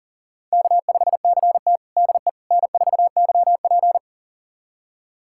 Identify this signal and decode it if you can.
Morse code, heard as a keyed tone, about 37 wpm, 700 Hz. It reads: K5CT DE N4YP
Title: USA Callsign DE Callsign - 4 Characters - 40wpm